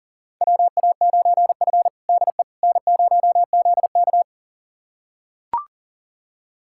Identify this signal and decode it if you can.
Morse code, heard as a keyed tone, about 40 wpm, 700 Hz. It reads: WA9F DE N0ZK
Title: USA Callsign DE Callsign - 4 Characters - 40wpm